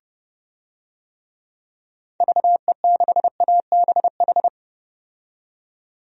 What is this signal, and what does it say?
Morse code, heard as a keyed tone, about 32 wpm, 700 Hz. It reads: VE6ABH
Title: DXCC-100 Call Signs - 6 Characters - 30wpm